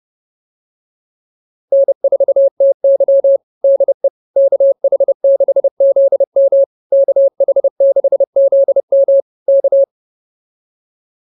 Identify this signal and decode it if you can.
Morse code, heard as a keyed tone, about 28 wpm, 560 Hz. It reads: N4TY DE KH6ZM KH6ZM K
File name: QSO109 30